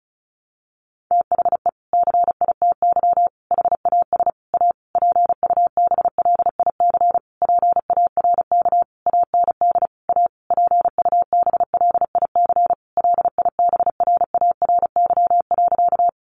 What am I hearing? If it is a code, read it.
Morse code, heard as a keyed tone, about 34 wpm, 700 Hz. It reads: THE CITY HAS A PUBLIC PARK AND A PUBLIC LIBRARY.
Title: Sentences from Top 1300 Words 35wpm